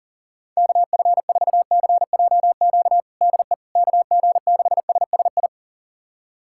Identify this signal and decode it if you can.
Morse code, heard as a keyed tone, about 37 wpm, 700 Hz. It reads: KF4CJQ DE KG6SSI
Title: USA Callsign DE Callsign - 6 Characters - 40wpm